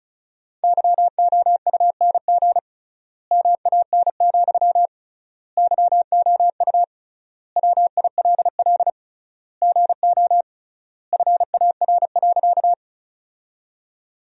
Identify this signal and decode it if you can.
Morse code, heard as a keyed tone, about 36 wpm, 700 Hz. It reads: YOUNG MAN, YOU WILL GO FAR.
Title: 3x Word-Spacing - Sentences from Top 400 Words - 35wpm